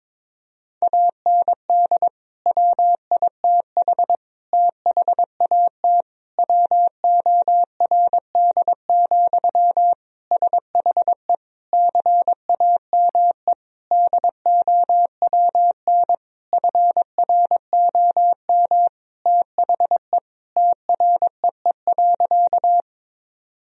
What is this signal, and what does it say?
Morse code, heard as a keyed tone, about 21 wpm, 700 Hz. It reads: AND WITH THAT WORD, SHE CAME DOWN FROM THE TREE.
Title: Sentences from Top 1000 Words - Review Part 1 - 22wpm